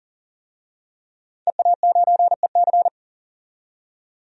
Morse code, heard as a keyed tone, about 40 wpm, 700 Hz. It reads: EA9EC
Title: DXCC-100 Call Signs - 5 Characters - 40wpm